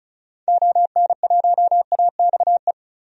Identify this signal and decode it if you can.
Morse code, heard as a keyed tone, about 36 wpm, 700 Hz. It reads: ON1AXE
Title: DXCC-100 Call Signs - 6 Characters - 35wpm